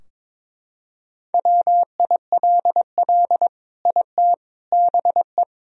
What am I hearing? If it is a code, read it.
Morse code, heard as a keyed tone, about 22 wpm, 700 Hz. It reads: WILL IT BE
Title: Sentences from Top 1000 Words - Review Part 2 - 22wpm